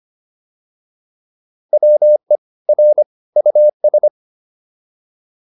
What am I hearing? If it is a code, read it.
Morse code, heard as a keyed tone, about 25 wpm, 600 Hz. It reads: WE R US